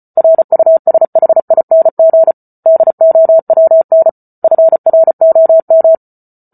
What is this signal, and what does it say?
Morse code, heard as a keyed tone, about 34 wpm, 650 Hz. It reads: RUSHING DOWN FROM